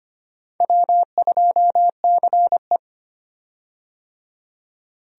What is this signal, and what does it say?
Morse code, heard as a keyed tone, about 25 wpm, 700 Hz. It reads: W2CE
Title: USA Call Signs - 4 Characters - 25wpm